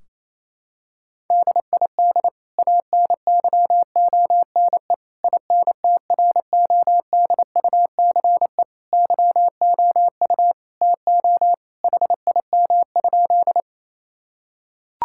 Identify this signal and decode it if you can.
Morse code, heard as a keyed tone, about 28 wpm, 700 Hz. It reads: DID ANYONE INTRODUCE YOU TO HIM?
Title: Sentences from Top 1600 Words 28wpm